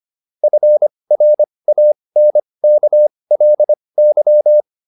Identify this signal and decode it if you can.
Morse code, heard as a keyed tone, about 25 wpm, spaced as for 20 wpm, 600 Hz. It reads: FRANKLY